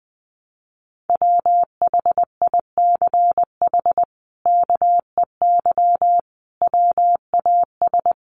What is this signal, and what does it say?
Morse code, heard as a keyed tone, about 20 wpm, 700 Hz. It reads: WHICH KEY WAS
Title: Sentences from Top 600 Words 20wpm